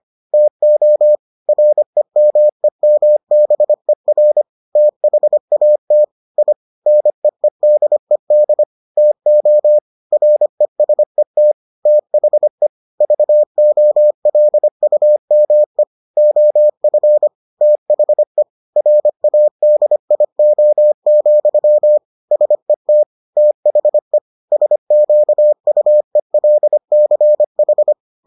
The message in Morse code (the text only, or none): TO REMEMBER THAT I NEEDED TO RESET THE VOLUME OF THE RADIO, SET THE SQUELCH